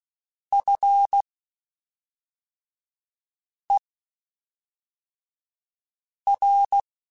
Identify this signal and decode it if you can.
Morse code, heard as a keyed tone, about 16 wpm, spaced as for 3 wpm, 800 Hz. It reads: FER
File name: (spaced) / QSOs Spaced-mp3s / kc7sac @5 sp